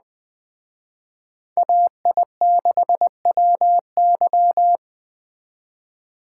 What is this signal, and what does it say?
Morse code, heard as a keyed tone, about 20 wpm, 700 Hz. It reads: AI6WY
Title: Speed Racing - USA 2x2 Call Signs 17wpm